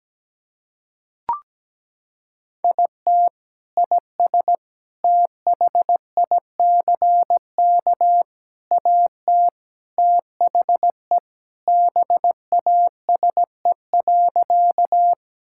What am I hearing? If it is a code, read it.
Morse code, heard as a keyed tone, about 17 wpm, 700 Hz. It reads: IT IS THICK AT THE BASE.
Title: Sentences from Top 1600 Words 17wpm